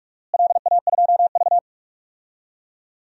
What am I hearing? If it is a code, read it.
Morse code, heard as a keyed tone, about 47 wpm, 700 Hz. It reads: LA2V
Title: DXCC-100 Call Signs - 4 Characters - Repeated 2x - 45wpm